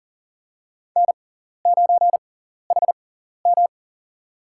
Morse code, heard as a keyed tone, about 40 wpm, spaced as for 14 wpm, 700 Hz. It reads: N9HM
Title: USA Call Signs - 4 Characters - 6x Character Spacing - 40wpm